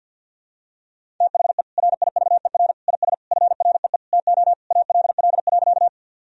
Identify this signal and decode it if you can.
Morse code, heard as a keyed tone, about 52 wpm, 700 Hz. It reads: THE RIVER IS FREE TO ALL.
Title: Sentences from Top 1000 Words - Review Part 1 - 50wpm